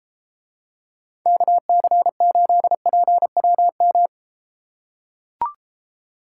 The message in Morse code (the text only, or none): KC8PWM